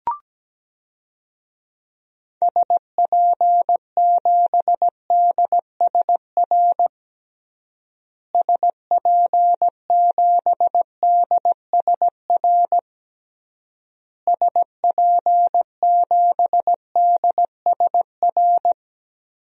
Morse code, heard as a keyed tone, about 17 wpm, 700 Hz. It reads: SP7DSR SP7DSR SP7DSR
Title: DXCC-100 Call Signs - 6 Characters - Repeated 3x - 17wpm